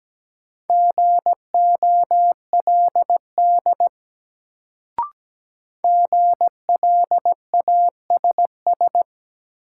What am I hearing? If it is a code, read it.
Morse code, heard as a keyed tone, about 17 wpm, 700 Hz. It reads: GOLD GLASS
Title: Sentences from Top 1000 Words - Review Part 3 - New Word Drill - 17wpm